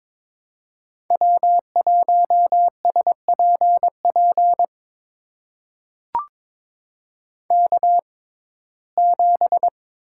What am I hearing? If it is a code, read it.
Morse code, heard as a keyed tone, about 22 wpm, spaced as for 16 wpm, 700 Hz. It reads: W1SPP K7
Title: USA Call Signs - 5 Characters - 6x Character Spacing - 22wpm